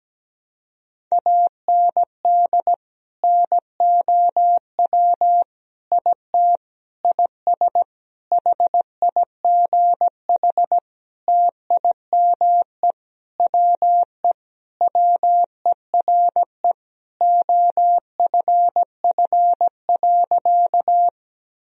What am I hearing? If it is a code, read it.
Morse code, heard as a keyed tone, about 17 wpm, 700 Hz. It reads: AND NOW IT IS HIGH TIME WE WERE OFF.
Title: Sentences from Top 1000 Words - Review Part 1 - 17wpm